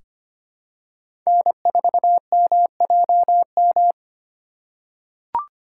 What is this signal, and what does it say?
Morse code, heard as a keyed tone, about 25 wpm, 700 Hz. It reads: N4MJM
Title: USA Call Signs - 2x Character Spacing - 25wpm